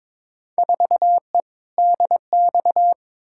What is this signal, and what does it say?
Morse code, heard as a keyed tone, about 22 wpm, 700 Hz. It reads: 4E DX
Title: Field Day Exchanges Encore - 22wpm